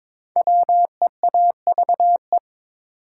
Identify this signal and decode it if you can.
Morse code, heard as a keyed tone, about 22 wpm, 700 Hz. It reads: WEAVE
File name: 5 Letter Words - First 3 Letters Given - 22wpm